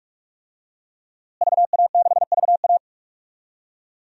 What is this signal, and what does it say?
Morse code, heard as a keyed tone, about 46 wpm, 700 Hz. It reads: VA6VA
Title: DXCC-100 Call Signs - 5 Characters - Repeated 3x - 45wpm